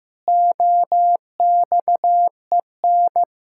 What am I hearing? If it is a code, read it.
Morse code, heard as a keyed tone, about 15 wpm, 700 Hz. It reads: OXEN
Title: Pangram Challenge v2 - 15wpm